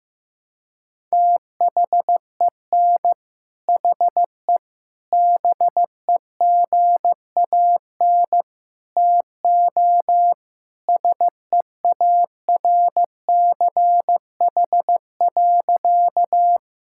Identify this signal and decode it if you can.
Morse code, heard as a keyed tone, about 15 wpm, 700 Hz. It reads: THEN HE BEGAN TO SEARCH.
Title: Sentences from Top 1000 Words - Review Part 5 - 15wpm